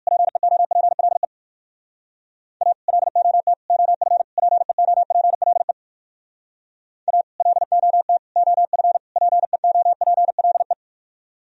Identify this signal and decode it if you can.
Morse code, heard as a keyed tone, about 58 wpm, 700 Hz. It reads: PEOPLE A LOT OF PEOPLE A LOT OF PEOPLE